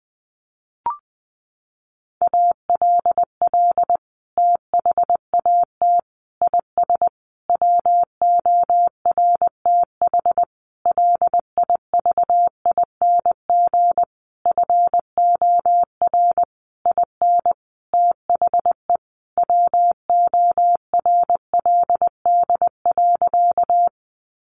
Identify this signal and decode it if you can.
Morse code, heard as a keyed tone, about 20 wpm, 700 Hz. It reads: ALL THAT IS WORTH LIVING FOR IN THE WORLD.
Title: Sentences from Top 1100 Words 20wpm